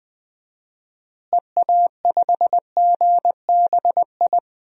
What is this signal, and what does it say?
Morse code, heard as a keyed tone, about 19 wpm, 700 Hz. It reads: EA5GBI
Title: DXCC-100 Call Signs - 6 Characters - Repeated 3x - 20wpm